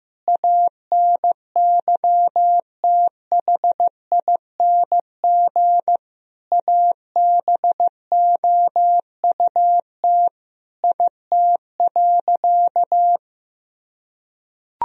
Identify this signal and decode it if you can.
Morse code, heard as a keyed tone, about 15 wpm, 700 Hz. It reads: ANYTHING ABOUT IT.
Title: Sentences from Top 1000 Words - Review Part 2 - 15wpm